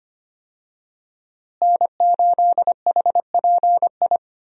Morse code, heard as a keyed tone, about 25 wpm, 700 Hz. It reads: N8HPI
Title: USA Call Signs - 5 Characters - Repeated 2x - 25wpm